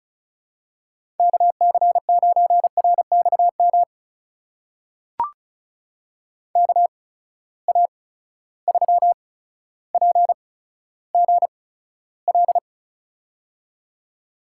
Morse code, heard as a keyed tone, about 36 wpm, spaced as for 15 wpm, 700 Hz. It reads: KC9RXM KA3PGL
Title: USA Call Signs - 6 Characters - 8x Character Spacing - 35wpm